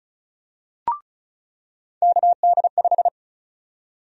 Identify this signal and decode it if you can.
Morse code, heard as a keyed tone, about 36 wpm, 700 Hz. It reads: KD5
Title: USA Call Sign Prefixes - 35wpm